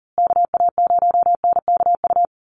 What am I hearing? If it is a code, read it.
Morse code, heard as a keyed tone, about 40 wpm, 700 Hz. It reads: KA0NKU
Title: USA Call Signs Repeated 3x 40wpm